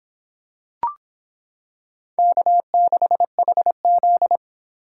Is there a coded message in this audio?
Yes: Morse code from 2.2 to 4.4 s, heard as a keyed tone, about 26 wpm, 700 Hz. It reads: K6HZ